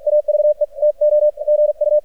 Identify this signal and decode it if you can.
Morse code, heard as a keyed tone, about 47 wpm, 600 Hz. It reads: AVE TO GU
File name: archived / CRSnet